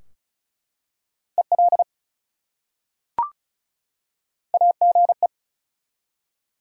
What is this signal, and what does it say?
Morse code, heard as a keyed tone, about 36 wpm, 700 Hz. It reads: EL AGE
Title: Top 180 QSO Elements 35wpm